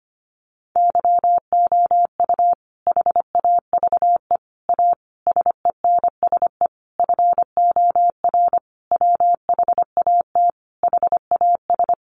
Morse code, heard as a keyed tone, about 25 wpm, 700 Hz. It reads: YOU HAVE A SENSE FOR WHAT HAS
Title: Sentences from Top 700 Words 25wpm